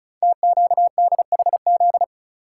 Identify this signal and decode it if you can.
Morse code, heard as a keyed tone, about 36 wpm, 700 Hz. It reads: TQDHZ
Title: Code Groups - 5 Letter:Numbers - 35wpm